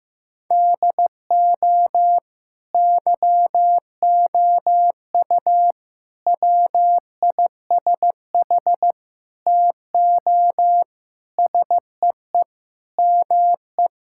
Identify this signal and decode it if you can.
Morse code, heard as a keyed tone, about 15 wpm, 700 Hz. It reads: DO YOU WISH TO SEE ME
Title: Sentences from Top 1600 Words 15wpm